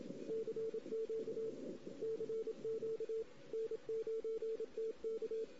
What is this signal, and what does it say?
Morse code, heard as a keyed tone, about 26 wpm, 450 Hz. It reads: CQ CQ N9TK
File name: CW / 2008-ss-cw-k5tr-op-n5rz -16:00